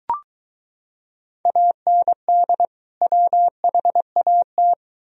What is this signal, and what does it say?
Morse code, heard as a keyed tone, about 23 wpm, 700 Hz. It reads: AND WHAT